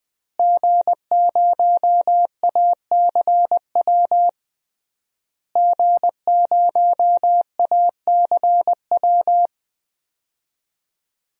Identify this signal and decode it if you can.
Morse code, heard as a keyed tone, about 21 wpm, 700 Hz. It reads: G0ACW G0ACW
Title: DXCC-100 Call Signs - 5 Characters - Repeated 3x - 20wpm